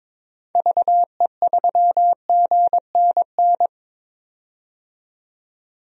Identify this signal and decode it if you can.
Morse code, heard as a keyed tone, about 22 wpm, 700 Hz. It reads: VE3GNN
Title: DXCC-100 Call Signs - 6 Characters - 22wpm